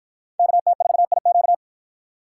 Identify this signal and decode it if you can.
Morse code, heard as a keyed tone, about 53 wpm, 700 Hz. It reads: KT4IQ